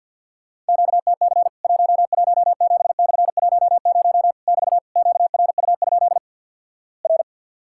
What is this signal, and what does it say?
Morse code, heard as a keyed tone, about 50 wpm, 700 Hz. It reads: QTC 117/10 <BT> QRU? R
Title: Worked All Europe DX Contest - QTC Exchange with Repeats - 50wpm